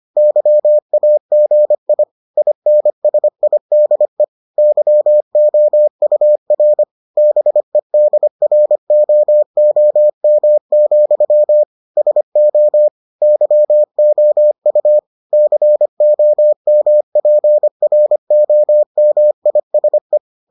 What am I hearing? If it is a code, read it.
Morse code, heard as a keyed tone, about 25 wpm, 600 Hz. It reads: YAGI INSIDE YOUR BEDROOM, SO YOU COMPROMISE